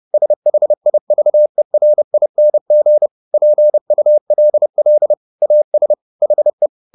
Morse code, heard as a keyed tone, about 30 wpm, 600 Hz. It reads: SHIVERING PULL AS HE
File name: the old man and the sea / the old man and the sea 1 30wpm